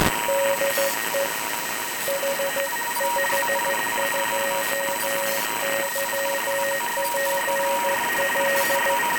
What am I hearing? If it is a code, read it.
Morse code, heard as a keyed tone, about 14 wpm, 550 Hz, with noise. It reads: DE H52YPL